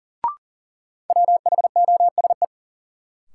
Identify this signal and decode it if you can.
Morse code, heard as a keyed tone, about 39 wpm, 700 Hz. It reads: WHOSE
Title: Sentences from Top 1200 Words - New Word Drill - 40wpm